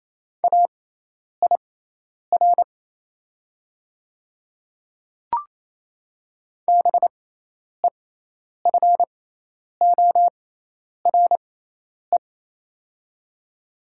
Morse code, heard as a keyed tone, about 27 wpm, spaced as for 10 wpm, 700 Hz. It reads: AIR BEFORE
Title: 6x Character Spacing - Top 200 Words - 28wpm